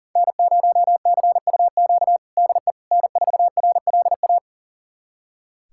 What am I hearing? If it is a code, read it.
Morse code, heard as a keyed tone, about 40 wpm, 700 Hz. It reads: N0CUQ DE N4RLA
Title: USA Callsign DE Callsign - 5 Characters - 40wpm